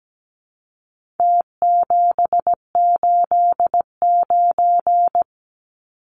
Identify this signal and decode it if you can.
Morse code, heard as a keyed tone, about 17 wpm, 700 Hz. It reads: T789
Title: Numbers - Contest Style - 4-Digit 17wpm